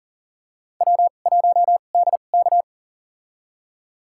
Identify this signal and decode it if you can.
Morse code, heard as a keyed tone, about 40 wpm, spaced as for 29 wpm, 700 Hz. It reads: W1DK
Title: USA Call Signs - 4 Characters - 2x Character Spacing - 40wpm